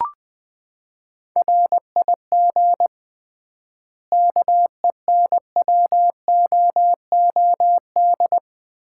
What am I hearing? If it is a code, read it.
Morse code, heard as a keyed tone, about 20 wpm, 700 Hz. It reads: RIG KENWOOD
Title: Common QSO Phrases - 3x Word-Spacing - 20wpm